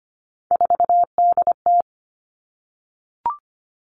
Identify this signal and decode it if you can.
Morse code, heard as a keyed tone, about 25 wpm, 700 Hz. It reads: 4DT
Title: Three Challenging Letter-Number Combinations 25wpm